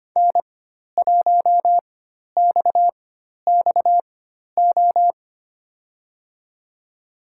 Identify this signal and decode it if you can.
Morse code, heard as a keyed tone, about 25 wpm, spaced as for 12 wpm, 700 Hz. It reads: N1XXO
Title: USA Call Signs - 5 Characters - 4x Character Spacing - 25wpm